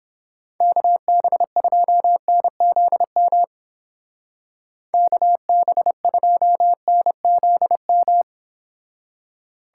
Morse code, heard as a keyed tone, about 30 wpm, 700 Hz. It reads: KB2NZM KB2NZM